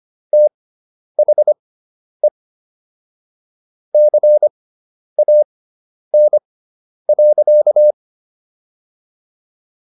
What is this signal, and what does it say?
Morse code, heard as a keyed tone, about 25 wpm, spaced as for 10 wpm, 600 Hz. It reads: THE CAN.